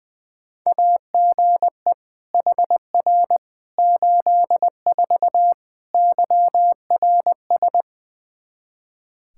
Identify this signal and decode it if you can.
Morse code, heard as a keyed tone, about 20 wpm, 700 Hz. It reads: AGE HR 84 YRS
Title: Practice QSO - Exchange 3 - More Info - 20wpm